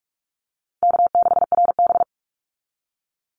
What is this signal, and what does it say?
Morse code, heard as a keyed tone, about 44 wpm, 700 Hz. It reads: K6RB
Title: Easy Call Signs Repeated 2x - 45wpm